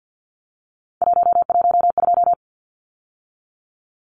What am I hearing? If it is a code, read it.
Morse code, heard as a keyed tone, about 50 wpm, 700 Hz. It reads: 112
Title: Numbers - 3-Digit 50wpm